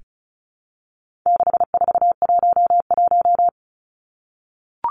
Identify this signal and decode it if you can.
Morse code, heard as a keyed tone, about 34 wpm, 700 Hz. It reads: 6411
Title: Numbers - 4-Digit 35wpm